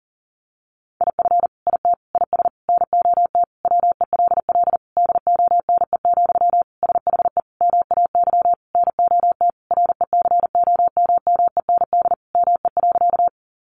Morse code, heard as a keyed tone, about 40 wpm, 700 Hz. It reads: IF IT IS NOT WELL DONE, SHE MAY NOT RECOMMEND ME.
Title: Sentences from Top 1200 Words 40wpm